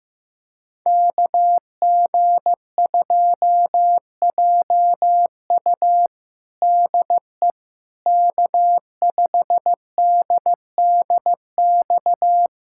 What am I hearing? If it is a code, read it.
Morse code, heard as a keyed tone, about 15 wpm, 700 Hz. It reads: KG2JU DE K5DDX
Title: USA Callsign DE Callsign - 5 Characters - 15wpm